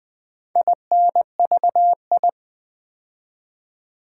Morse code, heard as a keyed tone, about 20 wpm, 700 Hz. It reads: INVI
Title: Build A Word - 2 Letters At A Time - 6-Letter Words - 20wpm